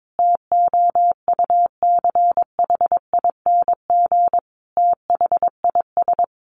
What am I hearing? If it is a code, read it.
Morse code, heard as a keyed tone, about 22 wpm, 700 Hz. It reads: TOUCHING THIS
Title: A Sound Of Thunder 22wpm